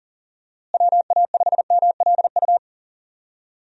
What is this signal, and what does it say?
Morse code, heard as a keyed tone, about 40 wpm, 700 Hz. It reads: WA5MLU
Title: USA Call Signs - 6 Characters - Repeated 3x - 40wpm